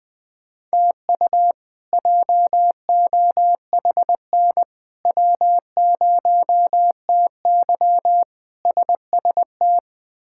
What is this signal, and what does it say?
Morse code, heard as a keyed tone, about 20 wpm, 700 Hz. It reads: TU JOHN W0TY SST
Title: K1USN SST QSO Practice - 20wpm